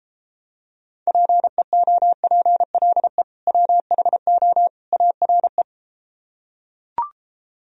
Morse code, heard as a keyed tone, about 33 wpm, 700 Hz. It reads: PEOPLE WHO ARE